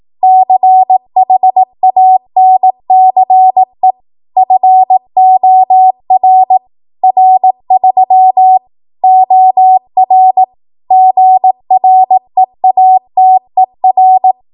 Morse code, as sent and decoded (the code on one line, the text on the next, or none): -.-. .... .- -. -.-. / ..-. --- .-. / .-. ...-- / --- .-. / --. .-. .- - . .-.
CHANCE FOR R3 OR GREATER